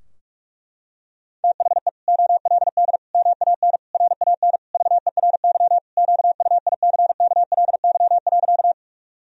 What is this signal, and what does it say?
Morse code, heard as a keyed tone, about 46 wpm, 700 Hz. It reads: THE OLD MAN RAN VERY QUICKLY.
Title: Sentences from Top 1000 Words - Review Part 3 - 45wpm